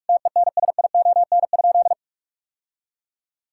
Morse code, heard as a keyed tone, about 46 wpm, 700 Hz. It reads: TENSION?
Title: Sentences from Top 1600 Words 45wpm